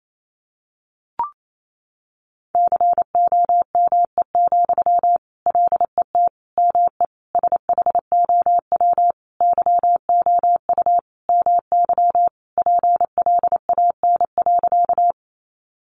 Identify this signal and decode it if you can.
Morse code, heard as a keyed tone, about 28 wpm, 700 Hz. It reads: COME, LET ME SHOW YOU MY PLAN.
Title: Sentences from Top 500 Words 28wpm